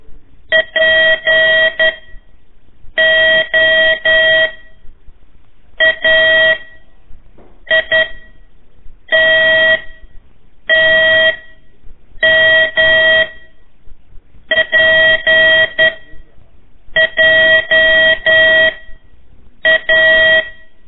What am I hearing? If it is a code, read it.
Morse code, heard as a keyed tone, about 11 wpm, 1900 Hz. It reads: P O A I T T M P J A
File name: 3 lezione CW 1 AIR